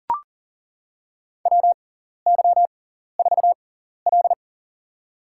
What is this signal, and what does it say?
Morse code, heard as a keyed tone, about 40 wpm, spaced as for 14 wpm, 700 Hz. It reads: WY4L
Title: USA Call Signs - 4 Characters - 6x Character Spacing - 40wpm